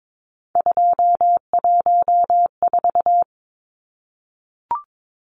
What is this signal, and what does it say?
Morse code, heard as a keyed tone, about 22 wpm, 700 Hz. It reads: 214
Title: Numbers - 3-Digit 22wpm